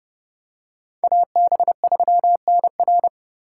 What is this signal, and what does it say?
Morse code, heard as a keyed tone, about 29 wpm, 700 Hz. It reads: AB3NR